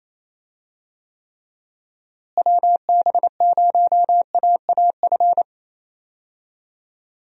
Morse code, heard as a keyed tone, about 28 wpm, 700 Hz. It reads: WB0AAF
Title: USA Call Signs - 6 Characters - Repeated 3x - 28wpm